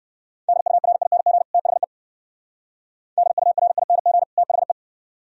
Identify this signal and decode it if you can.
Morse code, heard as a keyed tone, about 55 wpm, 700 Hz. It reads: DURING THE DURING THE
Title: Speed-Racing - Sets of 2 Words 45wpm